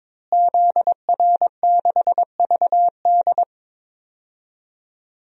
Morse code, heard as a keyed tone, about 22 wpm, 700 Hz. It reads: ZR6VD
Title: DXCC-100 Call Signs - 5 Characters - Repeated 2x - 22wpm